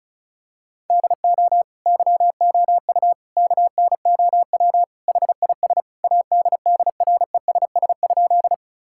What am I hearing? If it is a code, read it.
Morse code, heard as a keyed tone, about 36 wpm, 700 Hz. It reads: DO YOU KNOW HIS ADDRESS?
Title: Sentences from Top 1000 Words - Review Part 6 - 35wpm